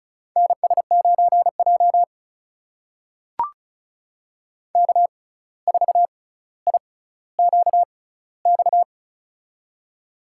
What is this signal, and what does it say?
Morse code, heard as a keyed tone, about 36 wpm, spaced as for 17 wpm, 700 Hz. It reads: NS9J K4IQX
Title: USA Call Signs - 6x Character Spacing - 35wpm